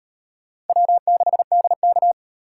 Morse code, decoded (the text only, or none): W6DK